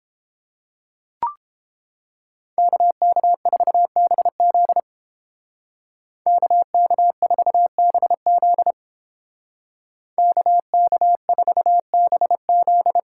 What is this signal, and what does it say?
Morse code, heard as a keyed tone, about 30 wpm, 700 Hz. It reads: KK4BZ KK4BZ KK4BZ